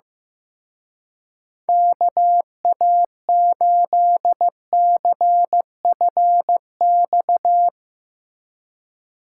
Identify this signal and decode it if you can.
Morse code, heard as a keyed tone, about 16 wpm, 700 Hz. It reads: KA8CFX
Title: Speed Racing - USA Call Signs 15wpm